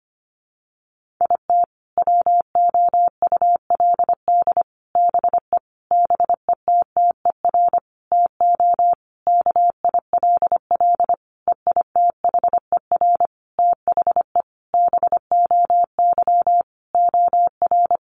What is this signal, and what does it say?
Morse code, heard as a keyed tone, about 25 wpm, 700 Hz. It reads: IT WOULD BE BETTER TO KILL EITHER THE BOY OR
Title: Sentences from Top 1500 Words 25wpm